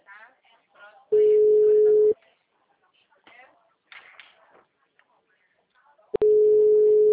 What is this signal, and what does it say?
Ringback tone (Europe, 425 Hz), with speech.